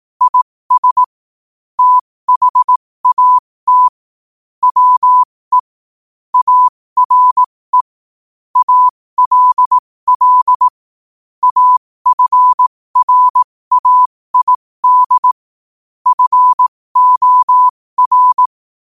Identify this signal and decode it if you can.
Morse code, heard as a keyed tone, about 18 wpm, spaced as for 15 wpm, 1000 Hz. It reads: IS THAT WE ARE ALL AFRAID FOR